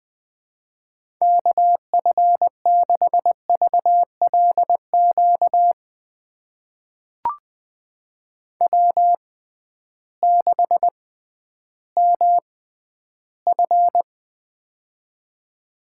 Morse code, heard as a keyed tone, about 21 wpm, spaced as for 12 wpm, 700 Hz. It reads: KF6VLQ W6MF
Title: USA Call Signs - 6x Character Spacing - 20wpm